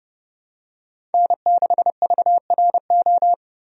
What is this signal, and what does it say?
Morse code, heard as a keyed tone, about 31 wpm, 700 Hz. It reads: N6VRO
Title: USA Call Signs - 5 Characters - Repeated 3x - 30wpm